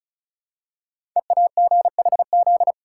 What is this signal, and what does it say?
Morse code, heard as a keyed tone, about 36 wpm, 700 Hz. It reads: EAGHZ
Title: Code Groups - 5 Letters - 35wpm